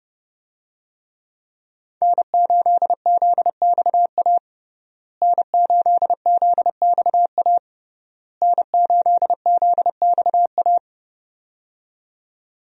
Morse code, heard as a keyed tone, about 30 wpm, 700 Hz. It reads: N8ZXA N8ZXA N8ZXA
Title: USA Call Signs - 5 Characters - Repeated 3x - 30wpm